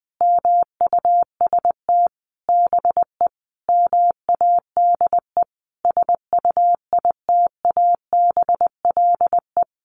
Morse code, heard as a keyed tone, about 20 wpm, 700 Hz. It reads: MUST BE MADE SUITABLE
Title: Sentences from Top 1400 Words 20wpm